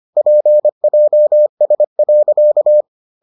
Morse code, heard as a keyed tone, about 25 wpm, 600 Hz. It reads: PJS.